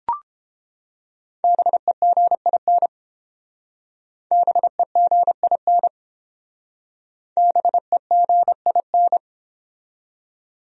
Morse code, heard as a keyed tone, about 29 wpm, 700 Hz. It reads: BEGIN BEGIN BEGIN